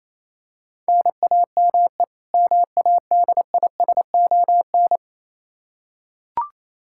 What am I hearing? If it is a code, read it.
Morse code, heard as a keyed tone, about 28 wpm, 700 Hz. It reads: NAME MADISON
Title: Practice QSO - Exchange 1 - Beginning - 28wpm